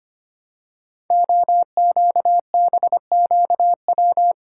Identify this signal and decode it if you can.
Morse code, heard as a keyed tone, about 25 wpm, 700 Hz. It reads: OQBQW
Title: Code Groups - 5 Letters - 25wpm